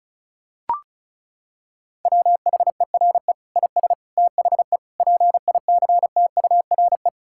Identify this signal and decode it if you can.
Morse code, heard as a keyed tone, about 34 wpm, 700 Hz. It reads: WHERE IS THE PICTURE
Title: Sentences from Top 1000 Words - Review Part 6 - 35wpm